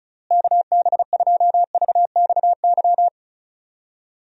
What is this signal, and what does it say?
Morse code, heard as a keyed tone, about 34 wpm, 700 Hz. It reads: KB2VXY
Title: USA Call Signs - 6 Characters - Repeated 3x - 35wpm